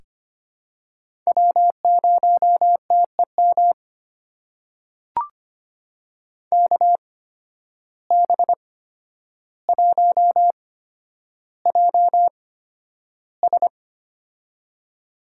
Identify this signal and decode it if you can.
Morse code, heard as a keyed tone, about 25 wpm, spaced as for 11 wpm, 700 Hz. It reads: W0TEM KB1JS
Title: USA Call Signs - 5 Characters - 8x Character Spacing - 25wpm